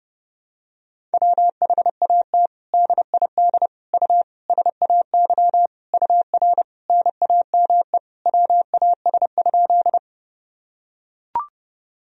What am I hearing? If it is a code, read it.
Morse code, heard as a keyed tone, about 30 wpm, 700 Hz. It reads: WHAT DID U SAY UR NAME WAS?
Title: Sentences from Top 200 Words and Abbreviations 30wpm